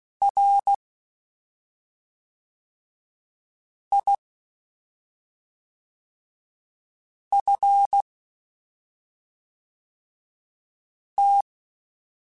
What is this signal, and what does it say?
Morse code, heard as a keyed tone, about 16 wpm, 800 Hz. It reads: R I F T
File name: Misc / Misc-mp3s / Jumps @5